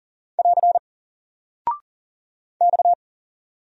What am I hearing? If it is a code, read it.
Morse code, heard as a keyed tone, about 39 wpm, 700 Hz. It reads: <AR> X
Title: Single Character (Letter:Number:Punctuation:Prosign) Alt 40wpm